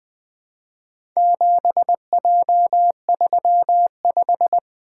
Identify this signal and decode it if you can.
Morse code, heard as a keyed tone, about 20 wpm, 700 Hz. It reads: 7J35
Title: Four Letter-Number Combinations - 20wpm